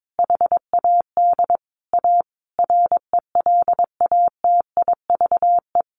Morse code, heard as a keyed tone, about 22 wpm, 700 Hz. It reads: HAD A RELATIVE
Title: Sentences from Top 1400 Words 22wpm